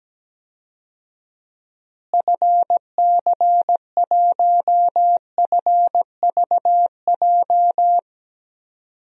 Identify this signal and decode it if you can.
Morse code, heard as a keyed tone, about 17 wpm, 700 Hz. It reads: FC1FVJ
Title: DXCC-100 Call Signs - 6 Characters - Repeated 2x - 17wpm